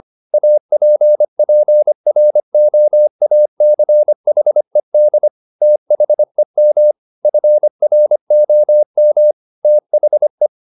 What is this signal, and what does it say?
Morse code, heard as a keyed tone, about 25 wpm, 600 Hz. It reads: APPROACHED THEM FROM THE
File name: the old man and the sea part 1 25wpm